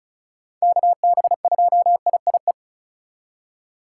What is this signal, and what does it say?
Morse code, heard as a keyed tone, about 34 wpm, 700 Hz. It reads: KB2IIE
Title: USA Call Signs - 6 Characters - 8x Character Spacing - 35wpm